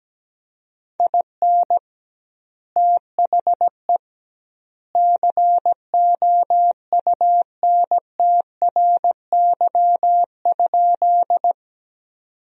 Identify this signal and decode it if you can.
Morse code, heard as a keyed tone, about 17 wpm, 700 Hz. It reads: IN THE COUNTRY?
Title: 2x Word-Spacing - Sentences from Top 300 Words - 17wpm